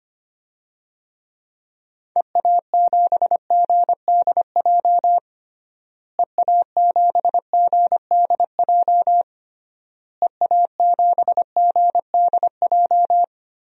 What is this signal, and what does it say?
Morse code, heard as a keyed tone, about 25 wpm, 700 Hz. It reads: EA7GDJ EA7GDJ EA7GDJ
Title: DXCC-100 Call Signs - 6 Characters - Repeated 3x - 25wpm